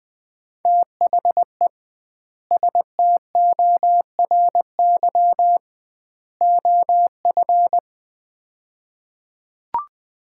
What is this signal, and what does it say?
Morse code, heard as a keyed tone, about 20 wpm, 700 Hz. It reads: THE STORY OF
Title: 2x Word-Spacing - Sets of 3 Words - 20wpm